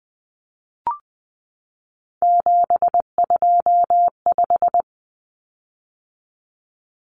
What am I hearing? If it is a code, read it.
Morse code, heard as a keyed tone, about 20 wpm, 700 Hz. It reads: 725
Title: Common Cut Numbers - 3-Digit 20wpm